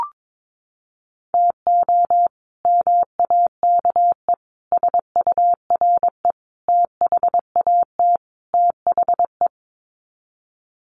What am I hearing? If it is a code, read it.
Morse code, heard as a keyed tone, about 22 wpm, 700 Hz. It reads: TO MAKE SURE THAT THE
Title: Sets of 5 Words 22wpm